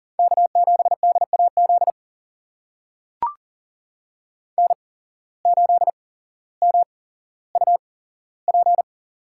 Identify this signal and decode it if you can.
Morse code, heard as a keyed tone, about 40 wpm, spaced as for 17 wpm, 700 Hz. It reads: K7DAZ N8MUP
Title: USA Call Signs - 5 Characters - 8x Character Spacing - 40wpm